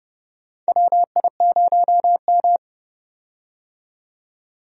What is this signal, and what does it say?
Morse code, heard as a keyed tone, about 31 wpm, 700 Hz. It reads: WI0M
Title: USA Call Signs - 4 Characters - 30wpm